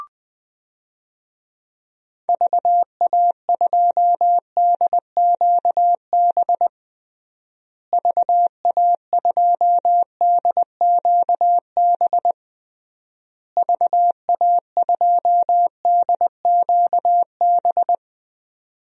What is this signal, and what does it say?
Morse code, heard as a keyed tone, about 20 wpm, 700 Hz. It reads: VA2DQB VA2DQB VA2DQB
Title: DXCC-100 Call Signs - 6 Characters - Repeated 3x - 20wpm